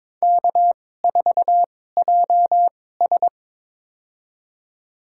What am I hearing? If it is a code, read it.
Morse code, heard as a keyed tone, about 22 wpm, spaced as for 16 wpm, 700 Hz. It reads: K4JS
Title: USA Call Signs - 4 Characters - 2x Character Spacing - 22wpm